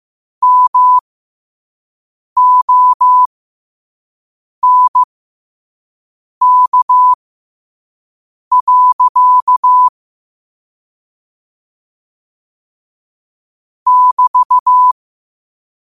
Morse code, heard as a keyed tone, about 15 wpm, spaced as for 5 wpm, 1000 Hz. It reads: MONK. <BT>
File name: 5wpm